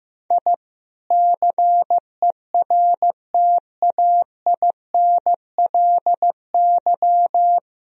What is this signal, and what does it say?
Morse code, heard as a keyed tone, about 15 wpm, 700 Hz. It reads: I CERTAINLY
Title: Sentences from Top 1000 Words - Review Part 4 - 15wpm